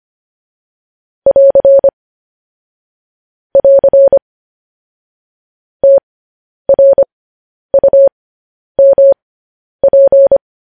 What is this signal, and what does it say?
Morse code, heard as a keyed tone, about 25 wpm, spaced as for 10 wpm, 550 Hz. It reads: <AR> <AR> TRUMP
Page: news headlines in morse code at 10 wpm